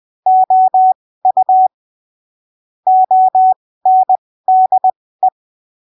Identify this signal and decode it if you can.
Morse code, heard as a keyed tone, about 20 wpm, spaced as for 15 wpm, 750 Hz. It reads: OU ONDE
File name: arquivos / texto / texto 13 20ppm